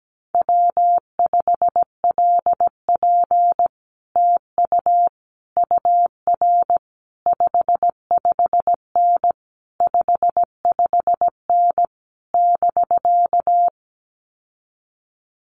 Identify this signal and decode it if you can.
Morse code, heard as a keyed tone, about 17 wpm, 700 Hz. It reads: W5LP TU UR 55N 55N <BK>
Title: Common QSO Phrases Encore 17wpm